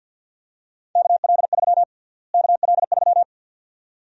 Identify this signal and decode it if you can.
Morse code, heard as a keyed tone, about 47 wpm, 700 Hz. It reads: KL3 KL3
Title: USA Call Sign Prefixes - Repeated 2x - 50wpm